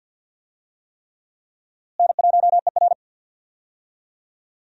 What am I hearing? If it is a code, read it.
Morse code, heard as a keyed tone, about 51 wpm, 700 Hz. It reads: N1ER
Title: USA Call Signs - 4 Characters - Repeated 3x - 50wpm